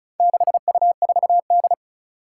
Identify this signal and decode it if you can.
Morse code, heard as a keyed tone, about 36 wpm, 700 Hz. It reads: 6U4D
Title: Four Challenging Letter-Number Combinations 35wpm